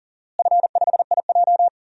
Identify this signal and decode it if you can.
Morse code, heard as a keyed tone, about 40 wpm, 700 Hz. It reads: F5IJ
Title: DXCC-100 Call Signs - 4 Characters - Repeated 2x - 40wpm